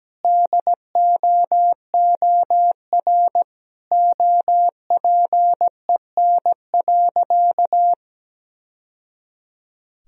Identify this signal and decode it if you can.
Morse code, heard as a keyed tone, about 17 wpm, 700 Hz. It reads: DOOR OPEN.
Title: Sentences from Top 1000 Words - Review Part 1 - 17wpm